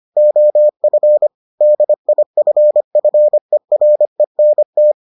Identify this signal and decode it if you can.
Morse code, heard as a keyed tone, about 25 wpm, 600 Hz. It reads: OF DIFFERENT